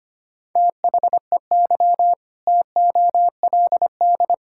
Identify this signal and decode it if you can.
Morse code, heard as a keyed tone, about 25 wpm, 700 Hz. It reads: THEY TOLD
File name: practice / 25wpm / Sentences with 2-Word Phrases - 25wpm